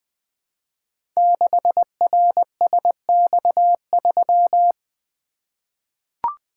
Morse code, heard as a keyed tone, about 20 wpm, 700 Hz. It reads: 6RSX3
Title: Code Groups - 5 Letter:Numbers - 20wpm